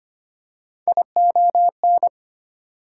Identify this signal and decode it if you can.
Morse code, heard as a keyed tone, about 25 wpm, 700 Hz. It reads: ION
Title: Prefixes and Suffixes 25wpm